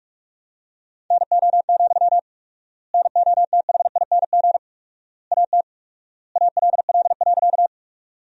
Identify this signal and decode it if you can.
Morse code, heard as a keyed tone, about 46 wpm, 700 Hz. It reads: NO, NOTHING AT ALL.
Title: 4x Word-Spacing - Sentences from Top 500 Words - 45wpm